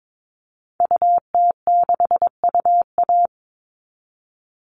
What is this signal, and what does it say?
Morse code, heard as a keyed tone, about 22 wpm, 700 Hz. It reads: UT6UA
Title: Call Signs Repeated 2x - 22wpm